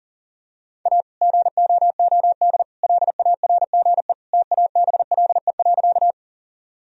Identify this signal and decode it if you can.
Morse code, heard as a keyed tone, about 37 wpm, 700 Hz. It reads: A GOOD LARGE TABLE.
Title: Sentences from Top 1000 Words - Review Part 2 - 40wpm